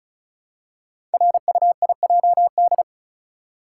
Morse code, heard as a keyed tone, about 36 wpm, 700 Hz. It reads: RUIJD
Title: Code Groups - 5 Letters - 35wpm